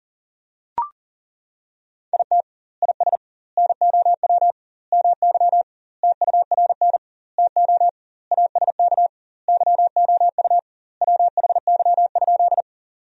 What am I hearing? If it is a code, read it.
Morse code, heard as a keyed tone, about 40 wpm, 700 Hz. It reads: IT IS NOW MY TURN TO ASK YOU WHY?
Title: 2x Word-Spacing - Sentences from Top 200 Words - 40wpm